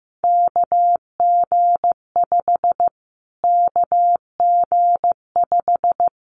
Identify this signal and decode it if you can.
Morse code, heard as a keyed tone, about 14 wpm, 700 Hz. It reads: KG5 KG5
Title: Callsign Prefixes for Top 25 DXCC - Repeated 3x - 15wpm